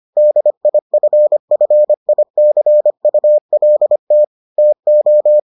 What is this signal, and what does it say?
Morse code, heard as a keyed tone, about 25 wpm, 600 Hz. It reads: DIFFICULT TO